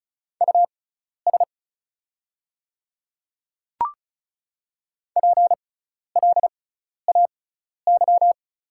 Morse code, heard as a keyed tone, about 36 wpm, spaced as for 12 wpm, 690 Hz. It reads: US PLAY
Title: 6x Character Spacing - Top 200 Words - 35wpm